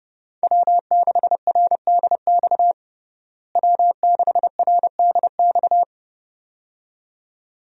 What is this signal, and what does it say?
Morse code, heard as a keyed tone, about 30 wpm, 700 Hz. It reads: W6RDX W6RDX
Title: USA Call Signs - 5 Characters - Repeated 2x - 30wpm